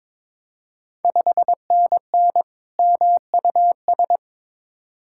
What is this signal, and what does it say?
Morse code, heard as a keyed tone, about 21 wpm, 690 Hz. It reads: 5NN MUS
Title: Ontario QSO Party Exchange 22wpm